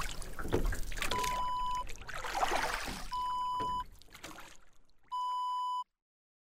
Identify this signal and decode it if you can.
There is no code.